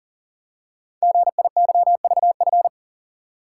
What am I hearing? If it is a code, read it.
Morse code, heard as a keyed tone, about 40 wpm, 700 Hz. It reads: GIYVF
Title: Code Groups - 5 Letters - 40wpm